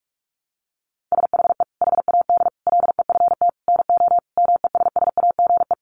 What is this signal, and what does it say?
Morse code, heard as a keyed tone, about 44 wpm, 700 Hz. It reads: SHE HAD LEFT NO MESSAGE
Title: Sentences from Top 1100 Words 45wpm